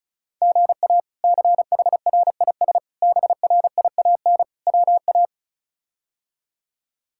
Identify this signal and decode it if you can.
Morse code, heard as a keyed tone, about 36 wpm, 700 Hz. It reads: GA CHRIS BRIAN WA
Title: K1USN SST QSO Practice - 35wpm